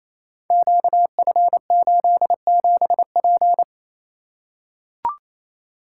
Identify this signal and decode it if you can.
Morse code, heard as a keyed tone, about 27 wpm, 700 Hz. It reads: QF87P
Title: Code Groups - 5 Letter:Numbers - 28wpm